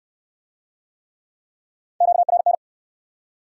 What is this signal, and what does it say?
Morse code, heard as a keyed tone, about 58 wpm, 700 Hz. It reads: QRN